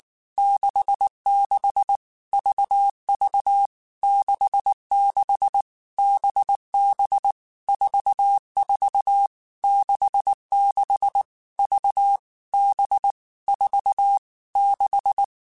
Morse code, heard as a keyed tone, about 19 wpm, 800 Hz. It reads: 66 VV 66 BB 44 66 V B 4 6
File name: Basic Letter Groups By7Groups / VB46 / VB46@18